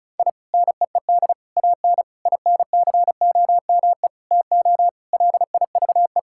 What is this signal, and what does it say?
Morse code, heard as a keyed tone, about 34 wpm, 700 Hz. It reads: I NEED AN INCOME TO LIVE
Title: Sentences from Top 1000 Words - Review Part 5 - 35wpm